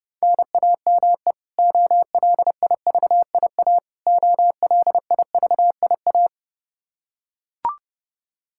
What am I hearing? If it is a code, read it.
Morse code, heard as a keyed tone, about 29 wpm, 700 Hz. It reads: NAME OLIVIA OLIVIA
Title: Practice QSO - Exchange 1 - Beginning - 30wpm